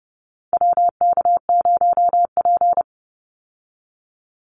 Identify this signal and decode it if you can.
Morse code, heard as a keyed tone, about 30 wpm, 700 Hz. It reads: WK0P
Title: Call Signs - 30wpm